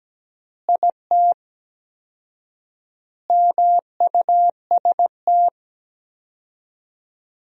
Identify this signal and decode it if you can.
Morse code, heard as a keyed tone, about 17 wpm, 700 Hz. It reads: IT MUST
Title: 4x Word-Spacing - Sentences from Top 500 Words - 17wpm